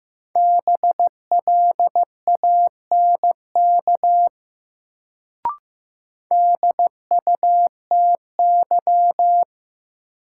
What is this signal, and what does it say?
Morse code, heard as a keyed tone, about 15 wpm, 700 Hz. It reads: BLANK DUTY
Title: Sentences from Top 1500 Words - New Word Drill Alt - 15wpm